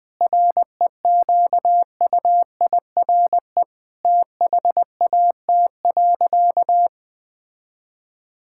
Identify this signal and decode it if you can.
Morse code, heard as a keyed tone, about 21 wpm, 700 Hz. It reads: REQUIRE THAT.
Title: Sentences from Top 1000 Words - Review Part 6 - 20wpm